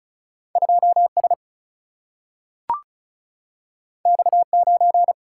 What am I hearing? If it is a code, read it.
Morse code, heard as a keyed tone, about 36 wpm, 700 Hz. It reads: 2S X9
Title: Single-Letter Single-Number 35wpm